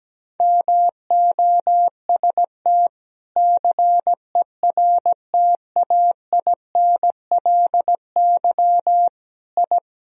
Morse code, heard as a keyed tone, about 17 wpm, 700 Hz. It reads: MOST CERTAINLY I
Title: Sentences from Top 1000 Words - Review Part 1 - 17wpm